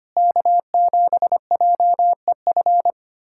Morse code, heard as a keyed tone, about 25 wpm, 700 Hz. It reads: K7JEF
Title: USA Callsign DE Callsign - 5 Characters - 25wpm